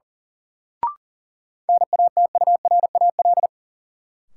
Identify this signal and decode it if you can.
Morse code, heard as a keyed tone, about 38 wpm, 700 Hz. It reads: NATURAL
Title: Sentences from Top 700 Words - New Word Drill - 40wpm